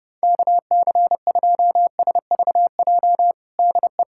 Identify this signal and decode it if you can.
Morse code, heard as a keyed tone, about 30 wpm, 700 Hz. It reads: KC2SVJ DE
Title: USA Callsign DE Callsign - 6 Characters - 30wpm